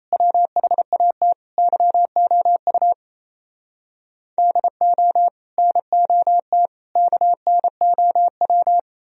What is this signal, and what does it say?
Morse code, heard as a keyed tone, about 37 wpm, 700 Hz. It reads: WHAT YOU DO NOT KNOW